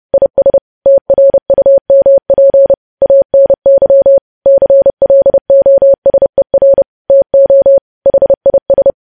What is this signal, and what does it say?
Morse code, heard as a keyed tone, about 30 wpm, 550 Hz. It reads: IS TRUMP ANY CLOSER TO HIS
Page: news headlines in morse code at 30 wpm